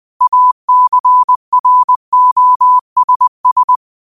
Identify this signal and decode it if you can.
Morse code, heard as a keyed tone, about 20 wpm, 1000 Hz. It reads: ACROSS